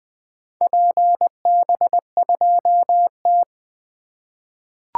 Morse code, heard as a keyed tone, about 20 wpm, 700 Hz. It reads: PB2T
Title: 2023 SOTA Activator Call Signs - 20wpm